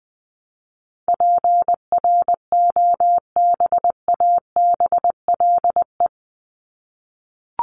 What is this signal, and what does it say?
Morse code, heard as a keyed tone, about 20 wpm, 700 Hz. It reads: PROBABLE
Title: LY Suffix - Top 100 Words with Base Words 20wpm